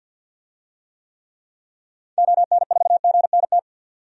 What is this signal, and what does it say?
Morse code, heard as a keyed tone, about 53 wpm, 700 Hz. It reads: ON4GNT
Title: DXCC-100 Call Signs - 6 Characters - 50wpm